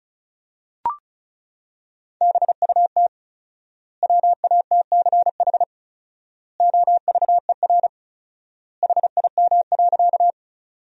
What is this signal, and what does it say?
Morse code, heard as a keyed tone, about 36 wpm, 700 Hz. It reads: BUT WATCH OVER HIM.
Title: 4x Word-Spacing - Sentences from Top 300 Words - 35wpm